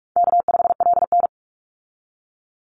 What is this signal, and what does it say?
Morse code, heard as a keyed tone, about 44 wpm, 700 Hz. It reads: K5LN
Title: Easy Call Signs Repeated 3x - 45wpm